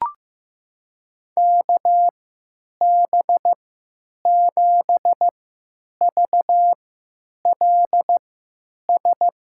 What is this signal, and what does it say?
Morse code, heard as a keyed tone, about 15 wpm, spaced as for 9 wpm, 700 Hz. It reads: KB7VLS
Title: USA Call Signs - 6 Characters - 3x Character Spacing - 15wpm